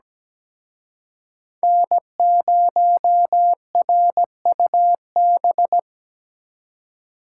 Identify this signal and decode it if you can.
Morse code, heard as a keyed tone, about 20 wpm, 700 Hz. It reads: N0RUB
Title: Speed Racing - USA Call Signs 17wpm